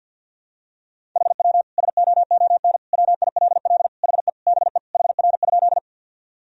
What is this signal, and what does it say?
Morse code, heard as a keyed tone, about 52 wpm, 700 Hz. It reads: HW SOON WILL HE BE HR?
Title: Sentences from Top 400 Words and Abbreviations 50wpm